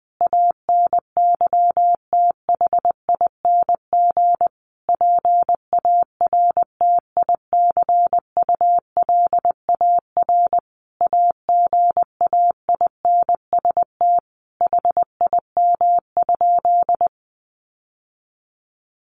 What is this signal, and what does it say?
Morse code, heard as a keyed tone, about 19 wpm, 700 Hz. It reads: ANYTHING PARTICULAR AGAINST HIM?
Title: Sentences from Top 1000 Words 20wpm